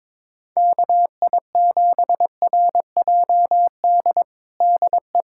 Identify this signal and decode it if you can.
Morse code, heard as a keyed tone, about 22 wpm, 700 Hz. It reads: KI7RJD DE
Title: USA Callsign DE Callsign - 6 Characters - 22wpm